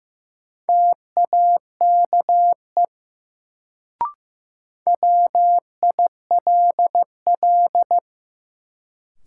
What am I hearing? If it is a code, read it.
Morse code, heard as a keyed tone, about 15 wpm, 700 Hz. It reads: TAKE WILL
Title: Sets of 2 Words - New Word Drill - 15wpm